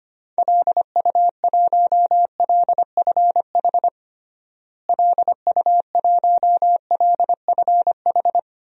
Morse code, heard as a keyed tone, about 25 wpm, 700 Hz. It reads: LU1LFH LU1LFH
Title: DXCC-100 Call Signs - 6 Characters - Repeated 3x - 25wpm